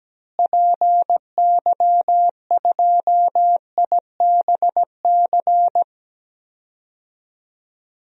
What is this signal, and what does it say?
Morse code, heard as a keyed tone, about 17 wpm, 700 Hz. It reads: PY2IBC
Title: DXCC-100 Call Signs - 6 Characters - Repeated 3x - 17wpm